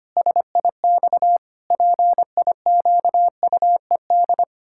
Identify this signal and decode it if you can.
Morse code, heard as a keyed tone, about 25 wpm, 700 Hz. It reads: SIX PIQUED
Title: Pangram Challenge v2 - 25wpm